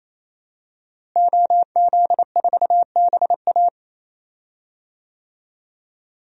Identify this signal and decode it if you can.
Morse code, heard as a keyed tone, about 27 wpm, 700 Hz. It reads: OZ4BA
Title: DXCC-100 Call Signs - 5 Characters - Repeated 3x - 28wpm